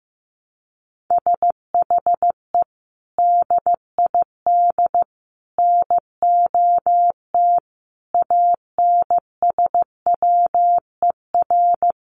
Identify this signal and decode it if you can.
Morse code, heard as a keyed tone, about 15 wpm, 700 Hz. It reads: SHE DID NOT ANSWER
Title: Sentences from Top 1100 Words 15wpm